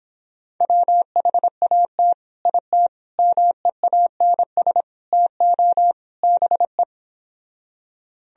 Morse code, heard as a keyed tone, about 26 wpm, 700 Hz. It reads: WHAT IT MEANS TO BE